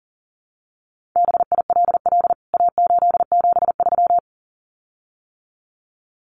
Morse code, heard as a keyed tone, about 40 wpm, 700 Hz. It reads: BILL A873
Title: CWT Exchange 40wpm